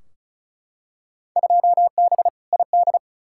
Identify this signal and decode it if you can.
Morse code, heard as a keyed tone, about 36 wpm, 700 Hz. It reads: 2B ID
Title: Field Day Exchanges Encore - 35wpm